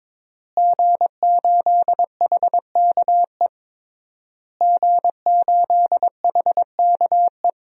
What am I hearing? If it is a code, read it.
Morse code, heard as a keyed tone, about 22 wpm, 700 Hz. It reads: G8HKE G8HKE
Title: DXCC-100 Call Signs - 5 Characters - Repeated 3x - 22wpm